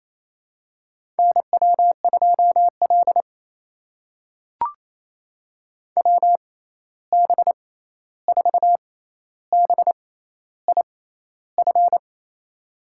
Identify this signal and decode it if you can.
Morse code, heard as a keyed tone, about 27 wpm, spaced as for 13 wpm, 700 Hz. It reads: NW2L WB4BIF
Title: USA Call Signs - 6x Character Spacing - 28wpm